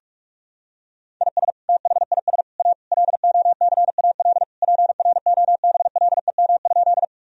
Morse code, heard as a keyed tone, about 43 wpm, 700 Hz. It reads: IS THIS A LOCAL PROBLEM?
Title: Sentences from Top 1000 Words - Review Part 3 - 45wpm